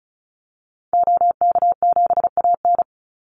Morse code, heard as a keyed tone, about 34 wpm, 700 Hz. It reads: OK7AN
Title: Call Signs Repeated 3x - 35wpm